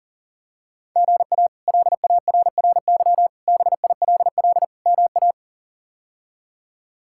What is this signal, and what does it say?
Morse code, heard as a keyed tone, about 40 wpm, 700 Hz. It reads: GA LARRY BILL MA
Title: K1USN SST QSO Practice - 40wpm